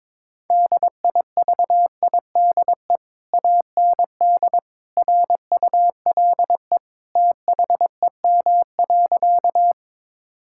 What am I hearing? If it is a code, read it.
Morse code, heard as a keyed tone, about 22 wpm, 700 Hz. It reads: DIVIDE AND RULE THEM.
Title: Sentences from Top 1600 Words 22wpm